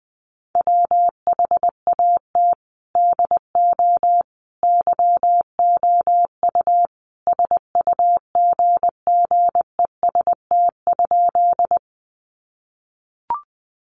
Morse code, heard as a keyed tone, about 20 wpm, 700 Hz. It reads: WHAT DO YOU SUGGEST?
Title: Sentences from Top 1300 Words 20wpm